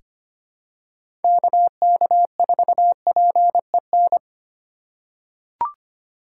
Morse code, heard as a keyed tone, about 25 wpm, 700 Hz. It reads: KK4PEN
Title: USA Call Signs - 8x Character Spacing - 25wpm